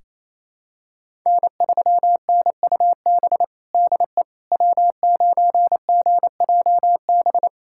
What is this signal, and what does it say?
Morse code, heard as a keyed tone, about 28 wpm, 700 Hz. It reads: N3NUB DE W9GJB
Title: USA Callsign DE Callsign - 5 Characters - 28wpm